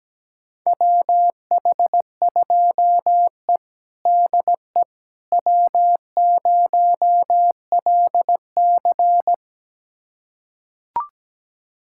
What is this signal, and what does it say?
Morse code, heard as a keyed tone, about 17 wpm, 700 Hz. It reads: WH2E DE W0LC
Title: USA Callsign DE Callsign - 4 Characters - 17wpm